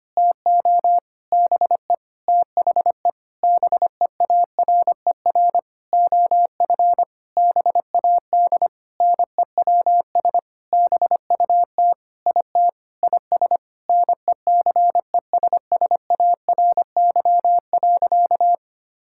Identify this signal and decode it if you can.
Morse code, heard as a keyed tone, about 25 wpm, 700 Hz. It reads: TO BE THE BEARER OF BAD NEWS BUT IT IS NECESSARY.
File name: practice / 25wpm / Sentences with 9-Word Phrases - 25wpm